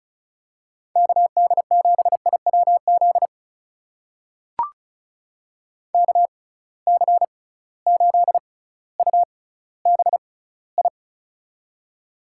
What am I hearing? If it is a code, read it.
Morse code, heard as a keyed tone, about 36 wpm, spaced as for 18 wpm, 700 Hz. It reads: KD7IWZ KC8UBI
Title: USA Call Signs - 6 Characters - 6x Character Spacing - 35wpm